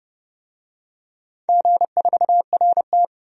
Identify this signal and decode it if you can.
Morse code, heard as a keyed tone, about 32 wpm, 700 Hz. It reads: G4RT
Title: DXCC-100 Call Signs - 4 Characters - 30wpm